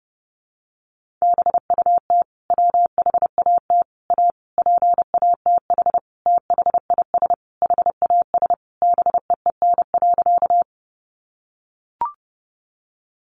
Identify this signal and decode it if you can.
Morse code, heard as a keyed tone, about 30 wpm, 700 Hz. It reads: BUT WHAT A PATH THIS HAS BEEN.
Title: Sentences from Top 1400 Words 30wpm